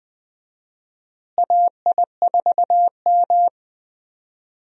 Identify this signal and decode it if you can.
Morse code, heard as a keyed tone, about 20 wpm, 700 Hz. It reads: AI4M
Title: USA Call Signs - 4 Characters - Repeated 3x - 20wpm